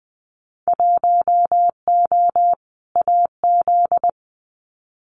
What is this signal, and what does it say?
Morse code, heard as a keyed tone, about 20 wpm, 700 Hz. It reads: 1O AZ
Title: Winter Field Day Exchanges 20wpm